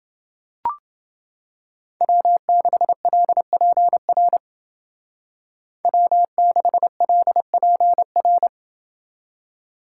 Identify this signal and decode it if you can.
Morse code, heard as a keyed tone, about 28 wpm, 700 Hz. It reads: W6LPR W6LPR